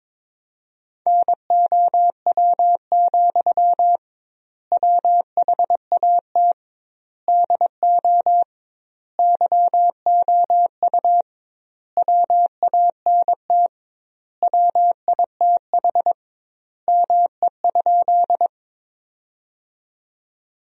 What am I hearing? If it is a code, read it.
Morse code, heard as a keyed tone, about 22 wpm, 700 Hz. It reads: NOW, WHAT DO YOU WANT WITH ME?
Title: 2x Word-Spacing - Sentences from Top 200 Words - 22wpm